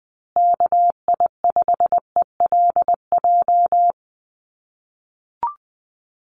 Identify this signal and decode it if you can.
Morse code, heard as a keyed tone, about 20 wpm, 700 Hz. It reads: KI5ELJ
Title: USA Call Signs Repeated 2x 20wpm